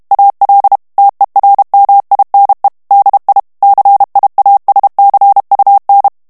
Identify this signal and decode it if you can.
Morse code, heard as a keyed tone, about 32 wpm, 800 Hz. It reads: AL TERMINE DI CIASCUN
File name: ITALIEN texte clair 2 vitesse 30 mots minute